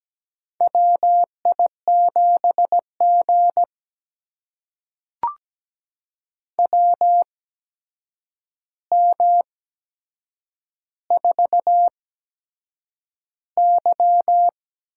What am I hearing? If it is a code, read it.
Morse code, heard as a keyed tone, about 17 wpm, spaced as for 7 wpm, 700 Hz. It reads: WI7G WM4Y
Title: USA Call Signs - 4 Characters - 8x Character Spacing - 17wpm